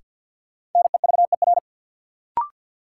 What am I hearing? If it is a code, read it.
Morse code, heard as a keyed tone, about 53 wpm, 700 Hz. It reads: NEVER
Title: Sentences from Top 300 Words - New Word Drill - 50wpm